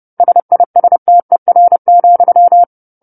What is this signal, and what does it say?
Morse code, heard as a keyed tone, about 30 wpm, 700 Hz. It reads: SISTER,
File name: 10-Chapter-10 30 WPM